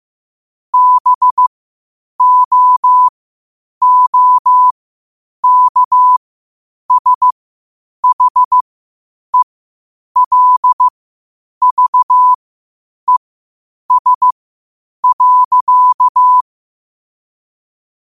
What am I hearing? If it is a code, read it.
Morse code, heard as a keyed tone, about 15 wpm, spaced as for 8 wpm, 1000 Hz. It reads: BOOKSHELVES.
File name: qotdfiles / 8wpm